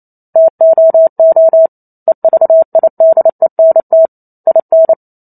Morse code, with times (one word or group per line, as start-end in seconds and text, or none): 0.4-1.7 TOO
2.1-4.1 EVIDENT
4.5-4.9 IN